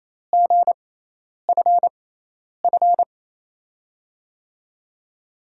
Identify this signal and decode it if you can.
Morse code, heard as a keyed tone, about 28 wpm, spaced as for 10 wpm, 700 Hz. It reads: GFF
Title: USA Call Signs - 6x Character Spacing - 28wpm